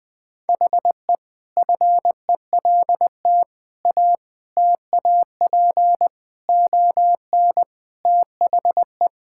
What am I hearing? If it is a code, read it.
Morse code, heard as a keyed tone, about 20 wpm, 700 Hz. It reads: HE FELT A TAP ON THE
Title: Sentences with 4-Word Phrases - 20wpm